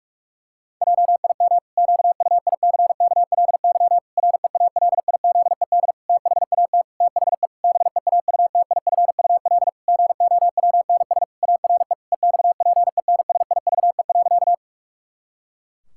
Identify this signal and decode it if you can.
Morse code, heard as a keyed tone, about 48 wpm, 700 Hz. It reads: JIM QUICKLY REALIZED THAT THE BEAUTIFUL GOWNS ARE EXPENSIVE.
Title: Pangram Challenge v2 - 45wpm